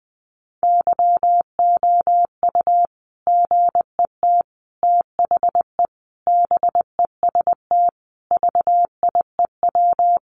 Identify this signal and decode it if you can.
Morse code, heard as a keyed tone, about 20 wpm, 700 Hz. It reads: YOU GET THE BEST VIEW
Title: Sentences from Top 1200 Words 20wpm